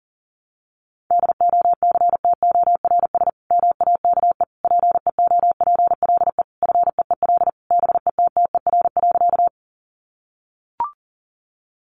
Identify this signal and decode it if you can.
Morse code, heard as a keyed tone, about 40 wpm, 700 Hz. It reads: DOCTORS MAKE PEOPLE FEEL BETTER.
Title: Easy Phrases 40wpm